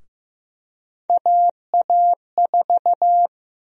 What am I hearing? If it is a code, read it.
Morse code, heard as a keyed tone, about 15 wpm, 700 Hz. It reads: AA4
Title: USA Call Sign Prefixes - Repeated 2x - 15wpm